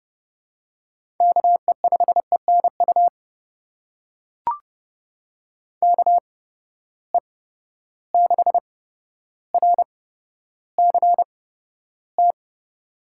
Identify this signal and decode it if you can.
Morse code, heard as a keyed tone, about 30 wpm, spaced as for 13 wpm, 700 Hz. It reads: KE5ENU KE6RCT
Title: USA Call Signs - 6 Characters - 8x Character Spacing - 30wpm